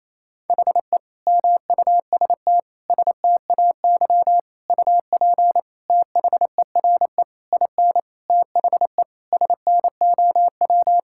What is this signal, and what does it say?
Morse code, heard as a keyed tone, about 28 wpm, 700 Hz. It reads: HE MUST STAY UP THERE IN THE SNOW
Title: Sentences from Top 1600 Words 28wpm